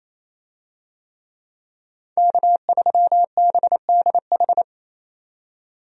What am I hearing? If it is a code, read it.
Morse code, heard as a keyed tone, about 28 wpm, 700 Hz. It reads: K3BDH
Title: USA Call Signs - 5 Characters - 28wpm